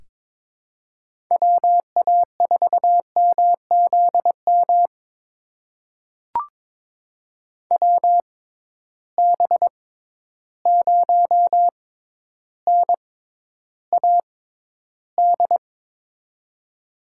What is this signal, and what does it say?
Morse code, heard as a keyed tone, about 22 wpm, spaced as for 11 wpm, 700 Hz. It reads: WA4MZM WB0NAD
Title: USA Call Signs - 6 Characters - 6x Character Spacing - 22wpm